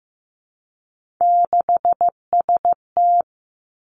Morse code, heard as a keyed tone, about 15 wpm, 700 Hz. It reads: 6ST
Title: Three Challenging Letter-Number Combinations 15wpm